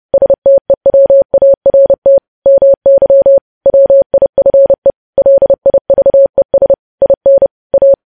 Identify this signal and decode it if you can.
Morse code, heard as a keyed tone, about 30 wpm, 550 Hz. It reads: STEWART MY WIFE LIVES IN A
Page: news headlines in morse code at 30 wpm